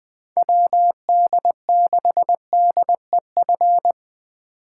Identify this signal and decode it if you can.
Morse code, heard as a keyed tone, about 20 wpm, 700 Hz. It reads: WD6DEF
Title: USA Call Signs - 2x Character Spacing - 20wpm